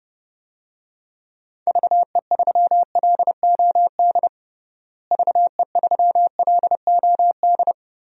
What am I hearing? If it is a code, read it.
Morse code, heard as a keyed tone, about 29 wpm, 700 Hz. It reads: VE3LOD VE3LOD
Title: DXCC-100 Call Signs - 6 Characters - Repeated 2x - 30wpm